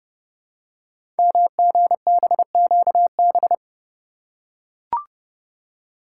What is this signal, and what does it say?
Morse code, heard as a keyed tone, about 30 wpm, 700 Hz. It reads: MGBQB
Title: Code Groups - 5 Letters - 30wpm